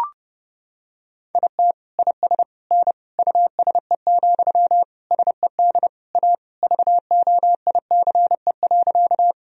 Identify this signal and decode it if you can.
Morse code, heard as a keyed tone, about 30 wpm, 700 Hz. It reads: IT IS N USE, SED A VOICE.
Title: Sentences from Top 500 Words and Abbreviations 30wpm